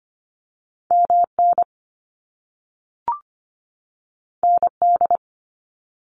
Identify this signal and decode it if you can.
Morse code, heard as a keyed tone, about 25 wpm, 700 Hz. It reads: MN ND
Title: US States 25wpm